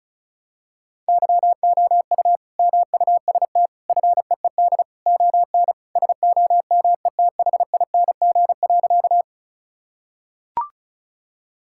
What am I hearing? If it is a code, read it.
Morse code, heard as a keyed tone, about 36 wpm, 700 Hz. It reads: YOU MUST FEED ON SOMETHING.
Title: Sentences from Top 1600 Words 35wpm